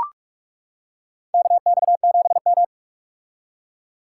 Morse code, heard as a keyed tone, about 41 wpm, 700 Hz. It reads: KX7M
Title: K1USN SST QSO Practice - 45wpm